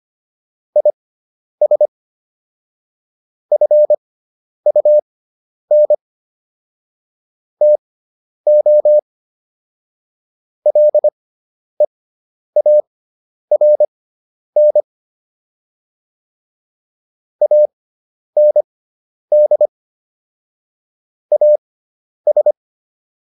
Morse code, heard as a keyed tone, about 25 wpm, spaced as for 10 wpm, 600 Hz. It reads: IS FUN TO LEARN AND AS